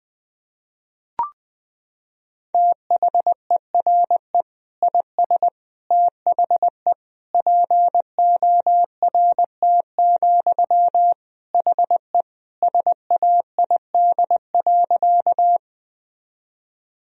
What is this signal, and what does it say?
Morse code, heard as a keyed tone, about 21 wpm, 700 Hz. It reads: THERE IS THE PORT, HE SAID.
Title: Sentences from Top 1000 Words - Review Part 1 - 20wpm